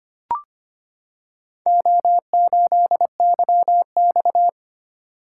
Morse code, heard as a keyed tone, about 25 wpm, 700 Hz. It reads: O8YX
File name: practice / 25wpm / Four Letter-Number Combinations - 25wpm